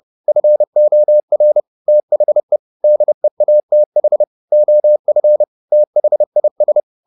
Morse code, heard as a keyed tone, about 30 wpm, 600 Hz. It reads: FOR THE DEATH OF THIS
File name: the old man and the sea / the old man and the sea 1 30wpm